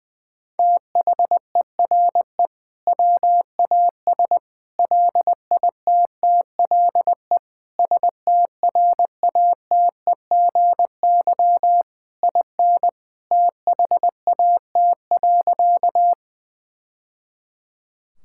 Morse code, heard as a keyed tone, about 21 wpm, 700 Hz. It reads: THERE WAS LITTLE STRATEGY IN THAT.
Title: Sentences from Top 1000 Words - Review Part 5 - 20wpm